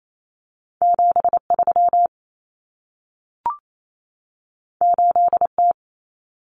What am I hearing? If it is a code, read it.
Morse code, heard as a keyed tone, about 28 wpm, 700 Hz. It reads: 73 8T
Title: Common Cut Numbers - 2-Digit 28wpm